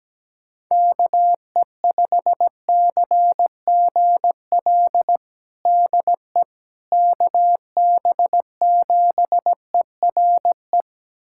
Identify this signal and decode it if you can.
Morse code, heard as a keyed tone, about 17 wpm, 700 Hz. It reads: KE5CGL DE KB7ERE
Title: USA Callsign DE Callsign - 6 Characters - 17wpm